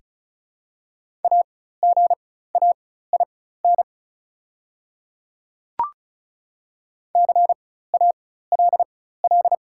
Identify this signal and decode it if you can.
Morse code, heard as a keyed tone, about 35 wpm, spaced as for 16 wpm, 700 Hz. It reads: AGAIN CALL
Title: 4x Character Spacing - Top 500 Words - 35wpm